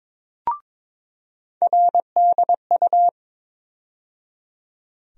Morse code, heard as a keyed tone, about 22 wpm, 700 Hz. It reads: RDU
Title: USA Call Sign Postfixes - 3 Letters 22wpm